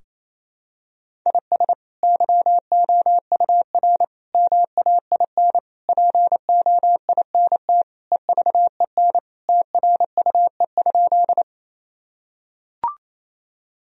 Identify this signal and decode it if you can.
Morse code, heard as a keyed tone, about 28 wpm, 700 Hz. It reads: IS YOUR MAIN POINT EVEN TRUE?
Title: Sentences from Top 1000 Words - Review Part 2 - 28wpm